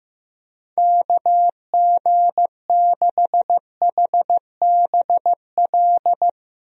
Morse code, heard as a keyed tone, about 15 wpm, 700 Hz. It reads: KG6HBL
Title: USA Call Signs - 6 Characters - 15wpm